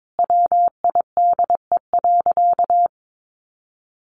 Morse code, heard as a keyed tone, about 22 wpm, 700 Hz. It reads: WIDE.
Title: Sentences from Top 1200 Words 22wpm